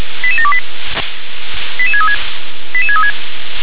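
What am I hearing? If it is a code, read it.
Five-tone selective call (ZVEI), heard as quick runs of five tones, tones of about 70 ms. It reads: 88728 88636 88636
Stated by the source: Received by SDR.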